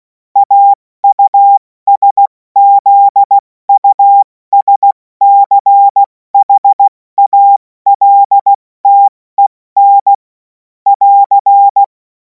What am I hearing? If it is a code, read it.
Morse code, heard as a keyed tone, about 16 wpm, 800 Hz. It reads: AUSZUSCHALTEN <AR>
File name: Morsekurs / Witze / Witz 28. April 2026 70 F80